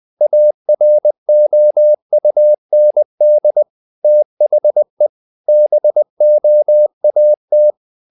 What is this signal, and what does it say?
Morse code, heard as a keyed tone, about 20 wpm, 600 Hz. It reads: AROUND THE BOAT
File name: the old man and the sea / the old man and the sea part 1 20wpm